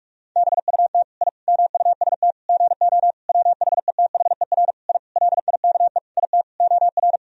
Morse code, heard as a keyed tone, about 47 wpm, 700 Hz. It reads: BUT I MUST GO WHETHER I LIKE IT OR
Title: Sentences from Top 1000 Words - Review Part 3 - 45wpm